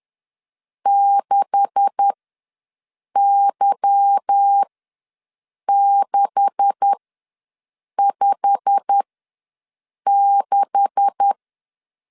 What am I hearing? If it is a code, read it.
Morse code, heard as a keyed tone, about 10 wpm, spaced as for 6 wpm, 800 Hz. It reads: 6Y656
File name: LEZIONI / 9 286 ws 10wpm 4032